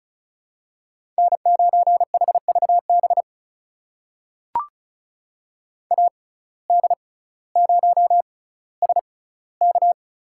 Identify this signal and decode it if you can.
Morse code, heard as a keyed tone, about 36 wpm, spaced as for 18 wpm, 700 Hz. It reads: N9HVB AD0SK
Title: USA Call Signs - 5 Characters - 6x Character Spacing - 35wpm